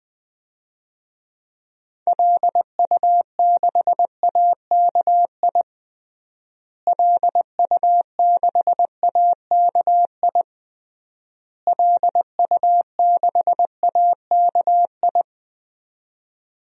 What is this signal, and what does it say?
Morse code, heard as a keyed tone, about 20 wpm, 700 Hz. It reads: LU6AKI LU6AKI LU6AKI
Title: DXCC-100 Call Signs - 6 Characters - Repeated 3x - 20wpm